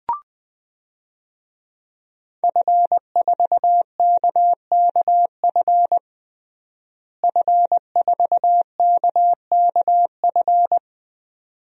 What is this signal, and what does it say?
Morse code, heard as a keyed tone, about 20 wpm, 700 Hz. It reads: F4KKF F4KKF
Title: DXCC-100 Call Signs - 5 Characters - Repeated 2x - 20wpm